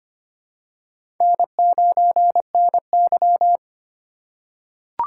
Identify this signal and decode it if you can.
Morse code, heard as a keyed tone, about 25 wpm, 700 Hz. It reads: N9NY
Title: USA Call Signs - 4 Characters - 2x Character Spacing - 25wpm